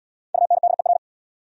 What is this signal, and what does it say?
Morse code, heard as a keyed tone, about 69 wpm, 700 Hz. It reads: UNDER